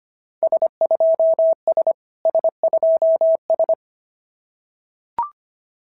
Morse code, heard as a keyed tone, about 25 wpm, 690 Hz. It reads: S2S S2S
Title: SOTA QSO Practice 25wpm